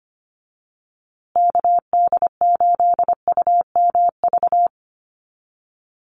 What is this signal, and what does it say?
Morse code, heard as a keyed tone, about 25 wpm, 700 Hz. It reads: KD8UMV
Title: USA Call Signs Repeated 2x 25wpm